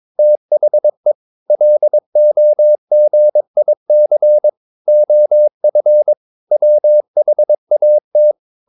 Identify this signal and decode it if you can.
Morse code, heard as a keyed tone, about 22 wpm, 600 Hz. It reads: THE LOGIC OF WHAT
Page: CW med Gnister